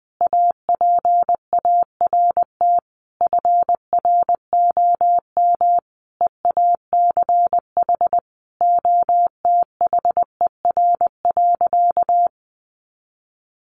Morse code, heard as a keyed tone, about 20 wpm, 700 Hz. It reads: APART FROM EACH OTHER.
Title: Easy Phrases 20wpm